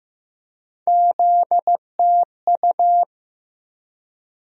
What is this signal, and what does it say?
Morse code, heard as a keyed tone, about 15 wpm, 700 Hz. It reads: ZTU
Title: USA Call Sign Postfixes - 3 Letters 15wpm